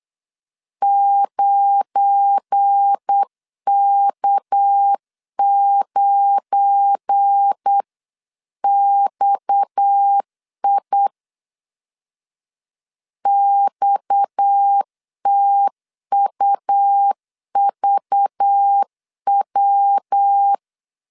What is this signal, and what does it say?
Morse code, heard as a keyed tone, about 8 wpm, 800 Hz. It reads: 9K9XI XTUVW
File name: LEZIONI / 7 GF9Z / 7 GF9 8wpm 3545